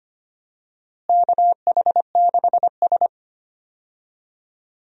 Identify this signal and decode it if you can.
Morse code, heard as a keyed tone, about 25 wpm, 700 Hz. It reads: KH6S
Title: DXCC-100 Call Signs - 4 Characters - Repeated 3x - 25wpm